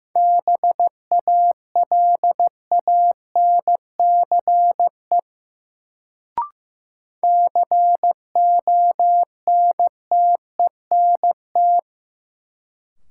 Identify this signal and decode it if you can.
Morse code, heard as a keyed tone, about 15 wpm, 700 Hz. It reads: BALANCE CONTENT
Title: Sentences from Top 900 Words - New Word Drill - 15wpm